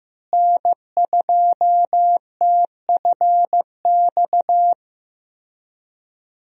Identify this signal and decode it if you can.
Morse code, heard as a keyed tone, about 16 wpm, 700 Hz. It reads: N2TFX
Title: USA Call Signs - 5 Characters - Repeated 2x - 15wpm